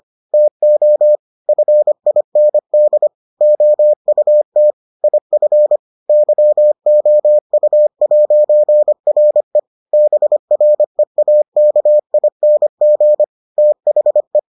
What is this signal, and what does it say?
Morse code, heard as a keyed tone, about 25 wpm, 600 Hz. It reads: TO FIND OUT IF YOU'RE BREAKING THE